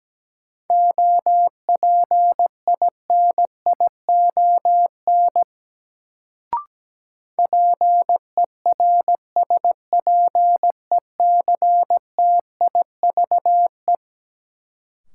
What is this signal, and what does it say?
Morse code, heard as a keyed tone, about 17 wpm, 700 Hz. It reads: OPINION PERSPECTIVE
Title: Sentences from Top 1200 Words - New Word Drill Alt - 17wpm